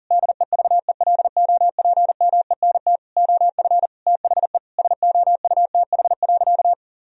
Morse code, heard as a keyed tone, about 40 wpm, 700 Hz. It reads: DEVELOPMENT OF THE SOUTH.
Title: Sentences from Top 1000 Words - Review Part 4 - 40wpm